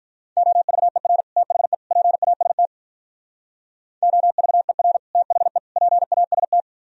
Morse code, heard as a keyed tone, about 44 wpm, 700 Hz. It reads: OVER THE PAST OVER THE PAST